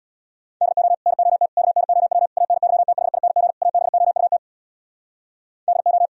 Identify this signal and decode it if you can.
Morse code, heard as a keyed tone, about 64 wpm, 700 Hz. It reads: DO NOT KNOW ANYTHING ABOUT DO